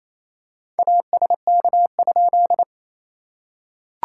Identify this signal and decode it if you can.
Morse code, heard as a keyed tone, about 27 wpm, 700 Hz. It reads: ASK?
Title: SOTA QSO Practice 28wpm